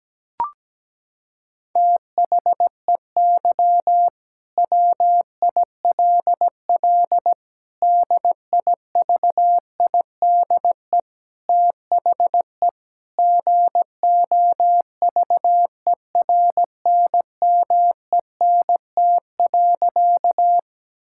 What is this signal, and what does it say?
Morse code, heard as a keyed tone, about 17 wpm, 700 Hz. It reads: THEY WILL DIVIDE THE GOVERNMENT.
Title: Sentences from Top 1600 Words 17wpm